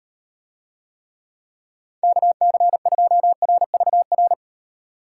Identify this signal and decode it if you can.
Morse code, heard as a keyed tone, about 38 wpm, 700 Hz. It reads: KC2RVR